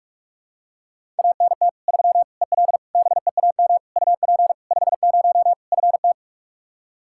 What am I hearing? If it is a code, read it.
Morse code, heard as a keyed tone, about 46 wpm, 700 Hz. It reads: ANT 3 EL BEAM UP 50 FT
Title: Practice QSO - Exchange 2 - Info - 45wpm